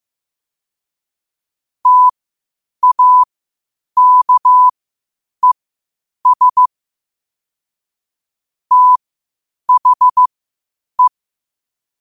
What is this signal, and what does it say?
Morse code, heard as a keyed tone, about 15 wpm, spaced as for 8 wpm, 1000 Hz. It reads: TAKES THE
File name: qotdfiles / 8wpm